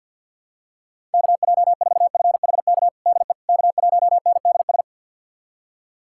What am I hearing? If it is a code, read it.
Morse code, heard as a keyed tone, about 52 wpm, 700 Hz. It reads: KJ4FHK DE K1NDS
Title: USA Callsign DE Callsign 50wpm